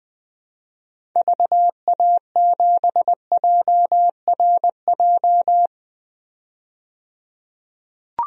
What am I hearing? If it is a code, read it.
Morse code, heard as a keyed tone, about 20 wpm, 700 Hz. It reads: VA7JRJ
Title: DXCC-100 Call Signs - 6 Characters - 20wpm